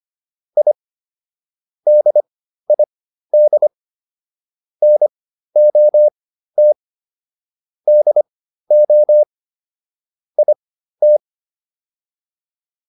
Morse code, heard as a keyed tone, about 25 wpm, spaced as for 13 wpm, 600 Hz. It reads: I DID NOT DO IT